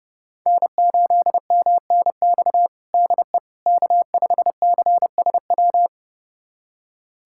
Morse code, heard as a keyed tone, about 30 wpm, 700 Hz. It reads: N8MNX DE K5CSW
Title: USA Callsign DE Callsign 30wpm